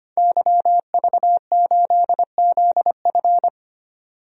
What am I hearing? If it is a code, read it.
Morse code, heard as a keyed tone, about 25 wpm, 700 Hz. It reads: YV8ZF
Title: DXCC-100 Call Signs - 5 Characters - 25wpm